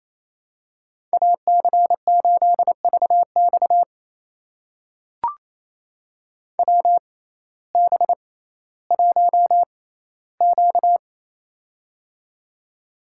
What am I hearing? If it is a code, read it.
Morse code, heard as a keyed tone, about 28 wpm, spaced as for 15 wpm, 700 Hz. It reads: AC8VX WB1Q
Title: USA Call Signs - 6x Character Spacing - 28wpm